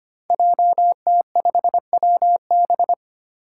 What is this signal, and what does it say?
Morse code, heard as a keyed tone, about 25 wpm, 700 Hz. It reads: JT5WB
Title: Five Challenging Letter-Number Combinations 25wpm